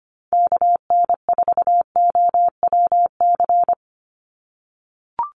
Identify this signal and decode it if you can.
Morse code, heard as a keyed tone, about 25 wpm, 700 Hz. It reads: KN4OWC
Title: USA Call Signs 25wpm